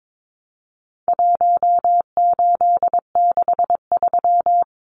Morse code, heard as a keyed tone, about 22 wpm, 700 Hz. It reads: 1863
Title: Numbers - Contest Style - 4-Digit 22wpm